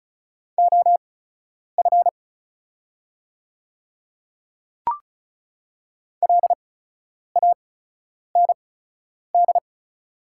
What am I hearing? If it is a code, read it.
Morse code, heard as a keyed tone, about 34 wpm, spaced as for 10 wpm, 700 Hz. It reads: OF LAND
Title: 8x Character Spacing - Top 300 Words - 35wpm